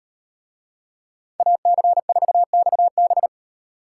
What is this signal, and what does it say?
Morse code, heard as a keyed tone, about 36 wpm, 700 Hz. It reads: AC4XB